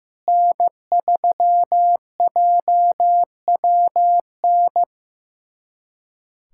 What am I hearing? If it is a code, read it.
Morse code, heard as a keyed tone, about 16 wpm, 700 Hz. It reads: N3JWN
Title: USA Callsign DE Callsign - 5 Characters - 15wpm